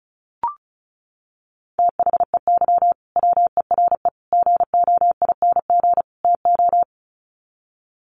Morse code, heard as a keyed tone, about 34 wpm, 700 Hz. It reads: THEY WERE GOING TO
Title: Sets of 4 Words 35wpm